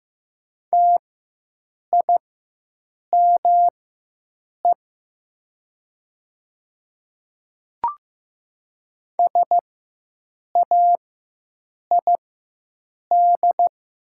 Morse code, heard as a keyed tone, about 15 wpm, spaced as for 7 wpm, 700 Hz. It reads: TIME SAID
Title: 4x Character Spacing - Top 100 Words - 15wpm